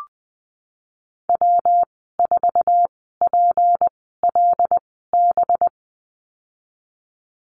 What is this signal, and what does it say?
Morse code, heard as a keyed tone, about 19 wpm, spaced as for 14 wpm, 700 Hz. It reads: W4PLB
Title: 207 Lesson 33g - Letter j - Call Signs 20wpm